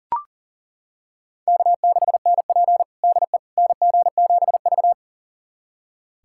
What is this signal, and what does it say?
Morse code, heard as a keyed tone, about 40 wpm, 700 Hz. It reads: K6NP DE NG7V
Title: USA Callsign DE Callsign - 4 Characters - 40wpm